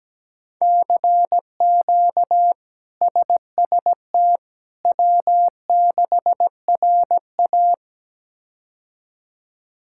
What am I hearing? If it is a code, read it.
Morse code, heard as a keyed tone, about 17 wpm, 700 Hz. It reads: CQ SST W6RA
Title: K1USN SST QSO Practice - 17wpm